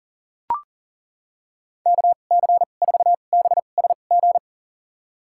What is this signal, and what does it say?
Morse code, heard as a keyed tone, about 40 wpm, spaced as for 29 wpm, 700 Hz. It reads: KC4BSG
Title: USA Call Signs - 6 Characters - 2x Character Spacing - 40wpm